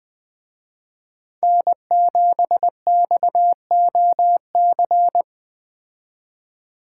Morse code, heard as a keyed tone, about 20 wpm, 700 Hz. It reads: N7XOC